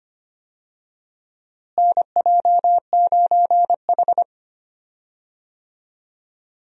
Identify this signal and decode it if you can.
Morse code, heard as a keyed tone, about 25 wpm, 700 Hz. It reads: NJ9H
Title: USA Call Signs - 4 Characters - Repeated 3x - 25wpm